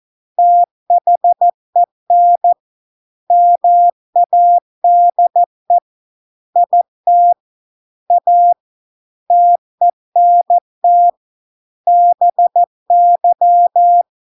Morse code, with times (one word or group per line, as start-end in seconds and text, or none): 0.4-2.5 THEN
3.3-5.8 MADE
6.6-7.3 IT
8.1-8.5 A
9.3-11.1 TENT
11.9-14.0 BY